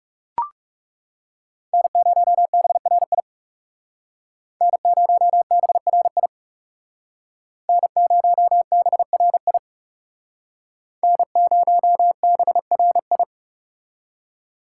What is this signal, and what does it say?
Morse code, heard as a keyed tone, about 38 wpm, 700 Hz. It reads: N0BRI N0BRI N0BRI N0BRI